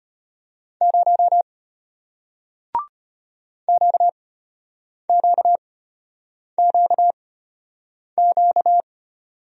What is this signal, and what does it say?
Morse code, heard as a keyed tone, about 33 wpm, spaced as for 8 wpm, 700 Hz. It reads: 0 QQQQ